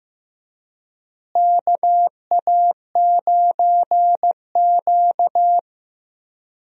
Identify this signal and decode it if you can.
Morse code, heard as a keyed tone, about 15 wpm, 700 Hz. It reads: KA9Q
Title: USA Call Signs - 4 Characters - Repeated 2x - 15wpm